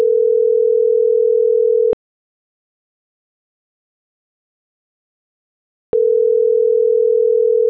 Ringback tone (North America), 440 + 480 Hz.